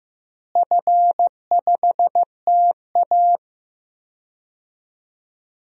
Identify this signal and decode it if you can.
Morse code, heard as a keyed tone, about 15 wpm, 700 Hz. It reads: F5TA
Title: DXCC-100 Call Signs - 4 Characters - Repeated 2x - 15wpm